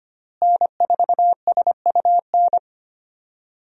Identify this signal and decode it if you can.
Morse code, heard as a keyed tone, about 25 wpm, 700 Hz. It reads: N4SUN
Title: USA Call Signs - 5 Characters - Repeated 3x - 25wpm